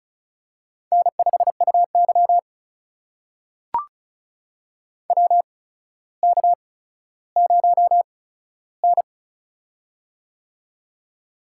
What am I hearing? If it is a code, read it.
Morse code, heard as a keyed tone, about 35 wpm, spaced as for 15 wpm, 700 Hz. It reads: N5UY WK0N
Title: USA Call Signs - 4 Characters - 8x Character Spacing - 35wpm